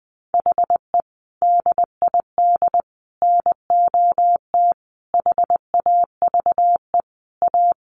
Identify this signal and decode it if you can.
Morse code, heard as a keyed tone, about 19 wpm, 700 Hz. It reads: HE DID NOT HAVE A
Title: Sentences from Top 1300 Words 20wpm